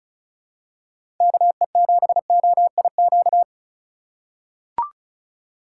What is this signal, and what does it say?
Morse code, heard as a keyed tone, about 34 wpm, 700 Hz. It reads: KE7OIQ
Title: USA Call Signs - 4x Character Spacing - 35wpm